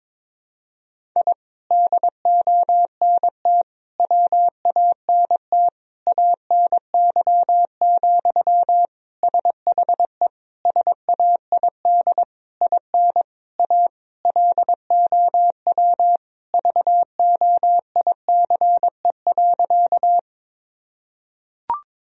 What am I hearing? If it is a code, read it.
Morse code, heard as a keyed tone, about 22 wpm, 700 Hz. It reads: I DONT WANT ANY, SHE SAID IN A LOW VOICE.
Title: Sentences from Top 1000 Words - Review Part 1 - 22wpm